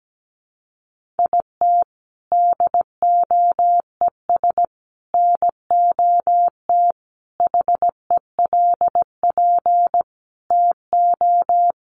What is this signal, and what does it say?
Morse code, heard as a keyed tone, about 17 wpm, 700 Hz. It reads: IT DOES NOT HELP TO
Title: Sentences from Top 1500 Words 17wpm